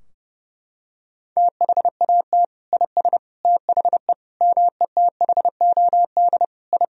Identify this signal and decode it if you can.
Morse code, heard as a keyed tone, about 30 wpm, 700 Hz. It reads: THAT IS THE METHOD I
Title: Sentences from Top 1000 Words - Review Part 3 - 30wpm